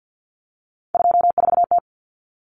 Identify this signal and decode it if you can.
Morse code, heard as a keyed tone, about 50 wpm, 700 Hz. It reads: 24T
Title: Common Cut Numbers - 3-Digit 50wpm